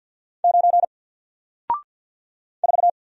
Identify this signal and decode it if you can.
Morse code, heard as a keyed tone, about 48 wpm, 700 Hz. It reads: 9 4
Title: Single Letter-Number Alt 50wpm